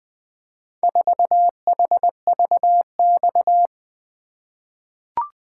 Morse code, heard as a keyed tone, about 20 wpm, 700 Hz. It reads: 4HVX
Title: Four Letter-Number Combinations - 20wpm